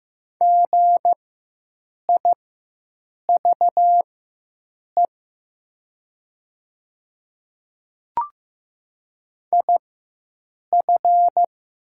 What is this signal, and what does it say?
Morse code, heard as a keyed tone, about 15 wpm, spaced as for 7 wpm, 700 Hz. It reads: GIVE IF
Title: 4x Character Spacing - Top 300 Words - 15wpm